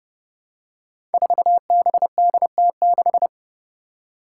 Morse code, heard as a keyed tone, about 30 wpm, 700 Hz. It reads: 4BDT6
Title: Five Challenging Letter-Number Combinations 30wpm